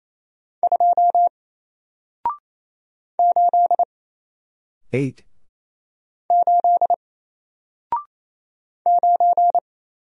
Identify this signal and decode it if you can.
Morse code, heard as a keyed tone, about 27 wpm, 700 Hz. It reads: 2 8 8 9
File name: Single-Letter-Number-28wpm